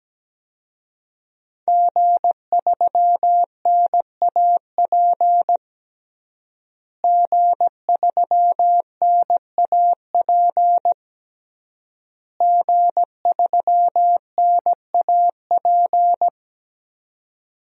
Morse code, heard as a keyed tone, about 17 wpm, 700 Hz. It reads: G3NAP G3NAP G3NAP
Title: DXCC-100 Call Signs - 5 Characters - Repeated 3x - 17wpm